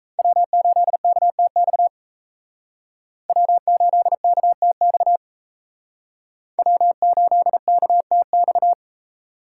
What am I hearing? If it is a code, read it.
Morse code, heard as a keyed tone, about 34 wpm, 700 Hz. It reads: W8KTX W8KTX W8KTX
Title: Speed Racing - USA 1x3 Call Signs 28wpm